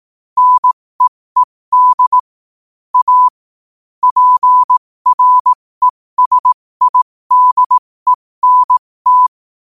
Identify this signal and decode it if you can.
Morse code, heard as a keyed tone, about 18 wpm, spaced as for 15 wpm, 1000 Hz. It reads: NEED A PRESIDENT
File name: qotdfiles / 15wpm